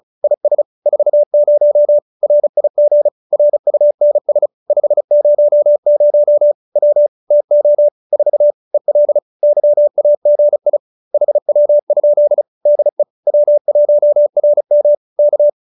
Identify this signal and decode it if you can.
Morse code, heard as a keyed tone, about 35 wpm, 590 Hz. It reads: IS 40 RIG RUNS 500 W TO 4 EL YAGI HW? DE W1RM K